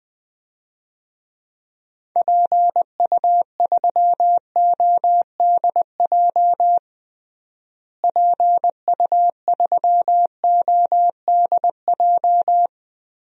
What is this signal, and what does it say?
Morse code, heard as a keyed tone, about 20 wpm, 700 Hz. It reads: PU3ODJ PU3ODJ
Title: DXCC-100 Call Signs - 6 Characters - Repeated 3x - 20wpm